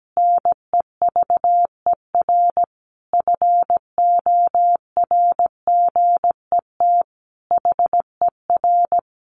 Morse code, heard as a keyed tone, about 17 wpm, 700 Hz. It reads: NEVER FORGET HER
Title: Additional Easy Phrases 17wpm